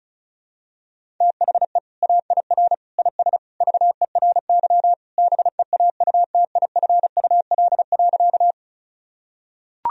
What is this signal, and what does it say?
Morse code, heard as a keyed tone, about 34 wpm, 700 Hz. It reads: THE AIR IS VERY BEAUTIFUL.
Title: Sentences from Top 1000 Words - Review Part 1 - 35wpm